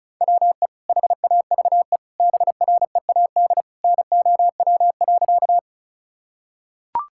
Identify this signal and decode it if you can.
Morse code, heard as a keyed tone, about 34 wpm, 700 Hz. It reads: WE HAVE BREAD NOW.
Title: Sentences from Top 1600 Words 35wpm